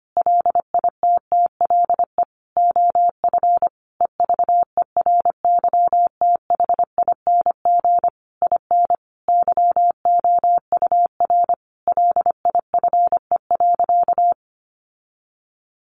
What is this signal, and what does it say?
Morse code, heard as a keyed tone, about 25 wpm, 700 Hz. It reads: LITTLE OF EVERYTHING IN YOUR LIFE.
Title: Sentences from Top 700 Words 25wpm